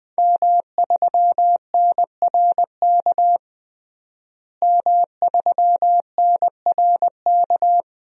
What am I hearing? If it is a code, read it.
Morse code, heard as a keyed tone, about 20 wpm, 700 Hz. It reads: M3NRK M3NRK
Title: DXCC-100 Call Signs - 5 Characters - Repeated 3x - 20wpm